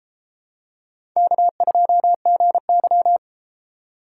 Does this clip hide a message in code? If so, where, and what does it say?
Yes: Morse code from 1.2 to 3.2 s, heard as a keyed tone, about 32 wpm, 700 Hz. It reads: K2GY